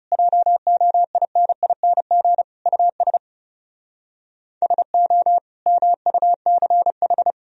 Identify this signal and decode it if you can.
Morse code, heard as a keyed tone, about 40 wpm, 700 Hz. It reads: JOINING US SO MUCH